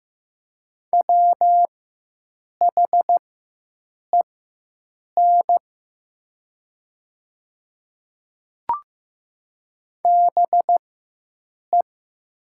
Morse code, heard as a keyed tone, about 15 wpm, spaced as for 7 wpm, 700 Hz. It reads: WHEN BE
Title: 4x Character Spacing - Top 100 Words - 15wpm